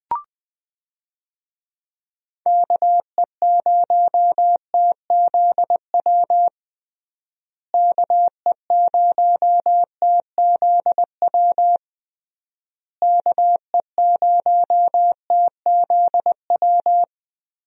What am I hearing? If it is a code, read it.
Morse code, heard as a keyed tone, about 21 wpm, 700 Hz. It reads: KE0TZW KE0TZW KE0TZW
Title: USA Call Signs - 6 Characters - Repeated 3x - 20wpm